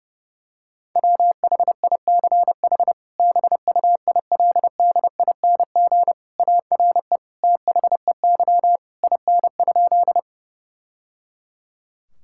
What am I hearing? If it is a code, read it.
Morse code, heard as a keyed tone, about 32 wpm, 700 Hz. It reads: WHICH BUILDING ARE THEY IN?
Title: Sentences from Top 1000 Words - Review Part 5 - 30wpm